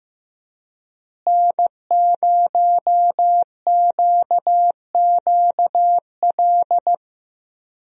Morse code, heard as a keyed tone, about 15 wpm, 700 Hz. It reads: N0QQL
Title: USA Call Signs - 5 Characters - Repeated 3x - 15wpm